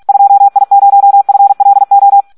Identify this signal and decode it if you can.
Morse code, heard as a keyed tone, about 46 wpm, 800 Hz. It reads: 2I0FLO
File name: Audio practica CW / SD 1 46